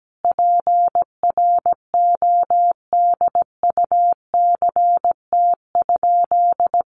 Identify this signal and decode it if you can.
Morse code, heard as a keyed tone, about 17 wpm, 700 Hz. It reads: PRODUCT?
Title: Sentences from Top 1400 Words 17wpm